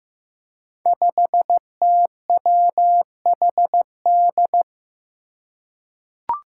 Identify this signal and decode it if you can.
Morse code, heard as a keyed tone, about 15 wpm, 700 Hz. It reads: 5TWHD
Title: Five Challenging Letter-Number Combinations 15wpm